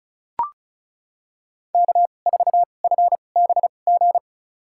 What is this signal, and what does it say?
Morse code, heard as a keyed tone, about 36 wpm, spaced as for 25 wpm, 700 Hz. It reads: K4FBG
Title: USA Call Signs - 5 Characters - 2x Character Spacing - 35wpm